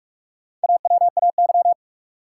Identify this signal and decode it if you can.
Morse code, heard as a keyed tone, about 46 wpm, 700 Hz. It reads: AWAY
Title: Sets of 2 Words - New Word Drill - 45wpm